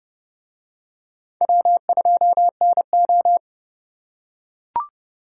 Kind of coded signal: Morse code

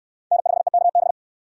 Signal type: Morse code

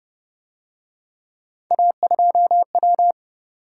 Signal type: Morse code

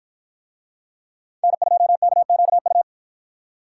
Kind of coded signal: Morse code

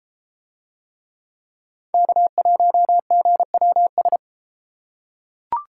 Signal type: Morse code

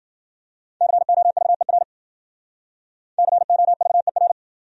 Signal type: Morse code

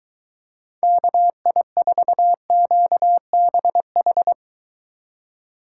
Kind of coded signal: Morse code

